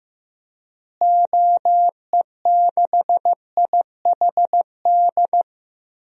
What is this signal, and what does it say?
Morse code, heard as a keyed tone, about 15 wpm, 700 Hz. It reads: OE6IHD
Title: DXCC-100 Call Signs - 6 Characters - Repeated 2x - 15wpm